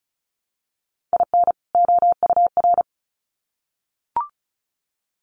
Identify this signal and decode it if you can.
Morse code, heard as a keyed tone, about 34 wpm, 700 Hz. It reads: IN OUR
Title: Sets of 2 Words 35wpm